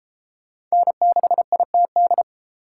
Morse code, heard as a keyed tone, about 31 wpm, 700 Hz. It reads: N6ITD